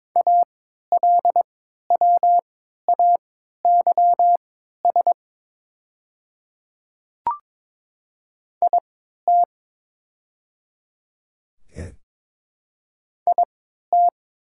Morse code, heard as a keyed tone, about 22 wpm, spaced as for 12 wpm, 700 Hz. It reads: ALWAYS IT IT
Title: 3x Character Spacing - Top 300 Words - 22wpm